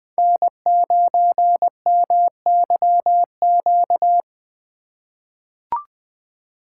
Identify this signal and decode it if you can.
Morse code, heard as a keyed tone, about 20 wpm, 700 Hz. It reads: N9MYQ
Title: USA Call Signs - 5 Characters - 2x Character Spacing - 20wpm